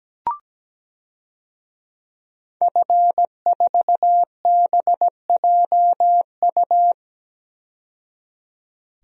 Morse code, heard as a keyed tone, about 17 wpm, 700 Hz. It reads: F4BJU
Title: DXCC-100 Call Signs - 5 Characters - 17wpm